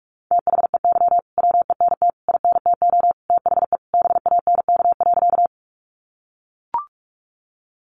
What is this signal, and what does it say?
Morse code, heard as a keyed tone, about 44 wpm, 700 Hz. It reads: THEY WENT INTO THE BANK.
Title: Sentences from Top 900 Words 45wpm